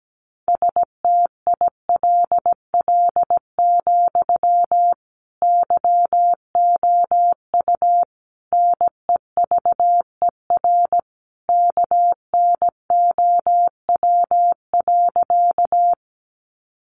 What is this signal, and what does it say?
Morse code, heard as a keyed tone, about 17 wpm, 700 Hz. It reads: STILL, YOU NEVER KNOW.
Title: Sentences from Top 300 Words 17wpm